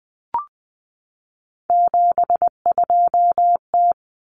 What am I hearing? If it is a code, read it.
Morse code, heard as a keyed tone, about 20 wpm, 700 Hz. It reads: 72T
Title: Common Cut Numbers - 3-Digit 20wpm